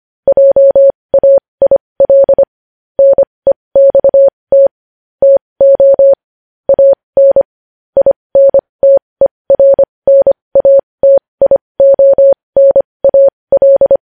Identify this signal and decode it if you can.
Morse code, heard as a keyed tone, about 25 wpm, spaced as for 20 wpm, 550 Hz. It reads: JAIL NEXT TO AN INTERNATIONAL
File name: podcast / News headlines-1x-25x20